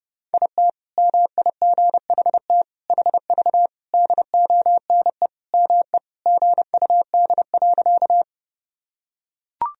Morse code, heard as a keyed tone, about 30 wpm, 700 Hz. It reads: IT MIGHT HV DONE ME GUD.
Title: Sentences from Top 500 Words and Abbreviations 30wpm